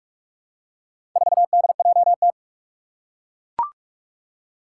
Morse code, heard as a keyed tone, about 41 wpm, 700 Hz. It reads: 4DJT
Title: Four Challenging Letter-Number Combinations 45wpm